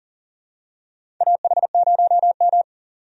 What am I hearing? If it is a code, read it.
Morse code, heard as a keyed tone, about 40 wpm, 700 Hz. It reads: AH0M
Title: USA Call Signs - 4 Characters - 40wpm